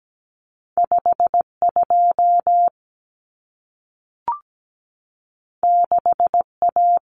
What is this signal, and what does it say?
Morse code, heard as a keyed tone, about 17 wpm, 700 Hz. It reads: 52 6A
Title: Common Cut Numbers - 2-Digit 17wpm